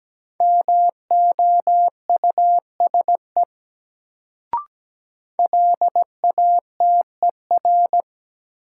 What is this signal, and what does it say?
Morse code, heard as a keyed tone, about 17 wpm, 700 Hz. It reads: MOUSE LATER
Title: Sentences from Top 700 Words - New Word Drill - 17wpm